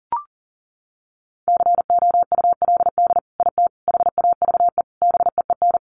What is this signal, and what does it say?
Morse code, heard as a keyed tone, about 40 wpm, 700 Hz. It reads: COULD IT HAVE BEEN
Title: Sentences from Top 1200 Words 40wpm